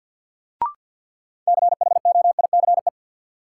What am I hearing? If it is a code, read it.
Morse code, heard as a keyed tone, about 53 wpm, 700 Hz. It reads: CHOICE
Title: Sentences from Top 900 Words - New Word Drill - 50wpm